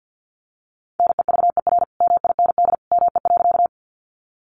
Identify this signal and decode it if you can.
Morse code, heard as a keyed tone, about 50 wpm, 700 Hz. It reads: NEVER MIND ME.
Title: Sentences from Top 500 Words 50wpm